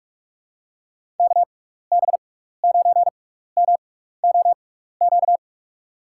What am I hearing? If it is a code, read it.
Morse code, heard as a keyed tone, about 47 wpm, spaced as for 16 wpm, 700 Hz. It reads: KB9MOQ
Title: USA Call Signs - 6 Characters - 6x Character Spacing - 45wpm